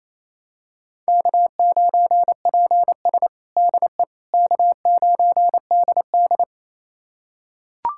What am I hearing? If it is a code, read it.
Morse code, heard as a keyed tone, about 28 wpm, 700 Hz. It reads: K9PS DE K9DD
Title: USA Callsign DE Callsign - 4 Characters - 28wpm